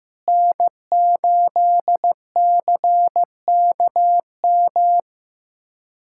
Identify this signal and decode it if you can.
Morse code, heard as a keyed tone, about 15 wpm, 700 Hz. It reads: N8CKM
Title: USA Callsign DE Callsign - 5 Characters - 15wpm